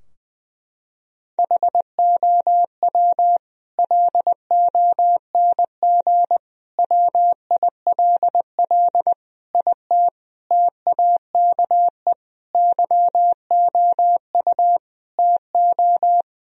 Morse code, heard as a keyed tone, about 20 wpm, 700 Hz. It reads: HOW LONG WILL IT TAKE YOU TO
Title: Sentences from Top 1000 Words - Review Part 2 - 20wpm